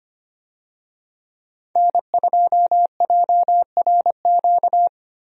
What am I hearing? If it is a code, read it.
Morse code, heard as a keyed tone, about 25 wpm, 700 Hz. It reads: N2JRQ
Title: USA Call Signs - 5 Characters - Repeated 3x - 25wpm